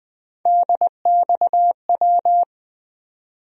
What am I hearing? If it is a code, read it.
Morse code, heard as a keyed tone, about 20 wpm, 700 Hz. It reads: DXW
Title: USA Call Sign Postfixes - 3 Letters 20wpm